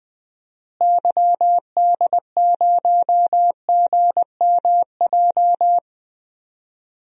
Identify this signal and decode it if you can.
Morse code, heard as a keyed tone, about 20 wpm, 700 Hz. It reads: YD0GMJ
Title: DXCC-100 Call Signs - 6 Characters - Repeated 2x - 20wpm